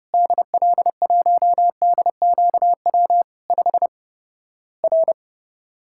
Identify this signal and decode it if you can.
Morse code, heard as a keyed tone, about 30 wpm, 700 Hz. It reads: DL1DQW 5 R
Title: Worked All Europe DX Contest - QTC Exchange - 30wpm